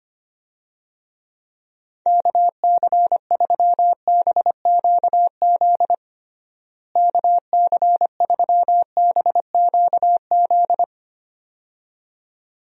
Morse code, heard as a keyed tone, about 25 wpm, 700 Hz. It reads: KC3BQZ KC3BQZ
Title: USA Call Signs - 6 Characters - Repeated 2x - 25wpm